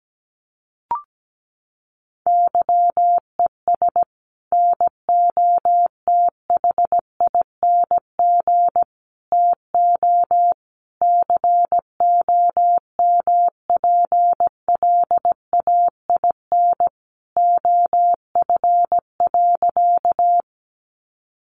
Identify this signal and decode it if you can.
Morse code, heard as a keyed tone, about 17 wpm, 700 Hz. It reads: YES NOTHING TO COMPLAIN OF.
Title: Sentences from Top 1300 Words 17wpm